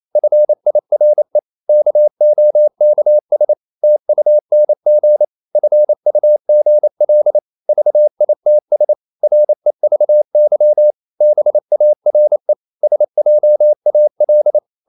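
Morse code, heard as a keyed tone, about 28 wpm, 600 Hz. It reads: FIRE KOKS TUNG FUGL VITS REVY BARE SJAL
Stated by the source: Danish short words sent in Morse at 28 wpm, from CW med Gnister.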